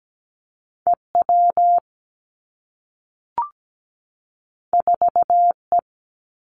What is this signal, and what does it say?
Morse code, heard as a keyed tone, about 17 wpm, 700 Hz. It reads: EW 4E
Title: Two Challenging Letter-Number Combinations 17wpm